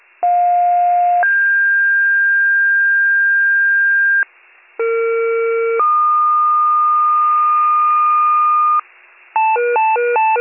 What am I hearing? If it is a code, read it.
Two-tone page, heard as a one-second tone, then a three-second tone, about 710 then 1600 Hz, about 460 then 1220 Hz, with noise. It is heard from 0.2 to 8.8 s.